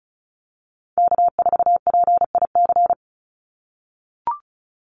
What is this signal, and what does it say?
Morse code, heard as a keyed tone, about 34 wpm, 700 Hz. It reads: K4PIC
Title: NA SOTA Chaser Callsigns 35wpm